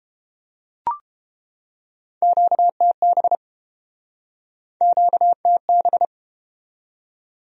Morse code, heard as a keyed tone, about 32 wpm, 700 Hz. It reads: QTB QTB